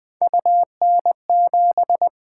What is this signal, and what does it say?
Morse code, heard as a keyed tone, about 20 wpm, 700 Hz. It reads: UN7
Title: Three Letter-Number Combinations - 20wpm